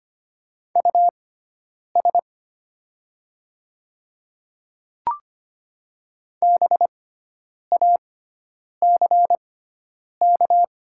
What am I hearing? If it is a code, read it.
Morse code, heard as a keyed tone, about 25 wpm, spaced as for 9 wpm, 690 Hz. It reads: US BACK
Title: 6x Character Spacing - Top 200 Words - 25wpm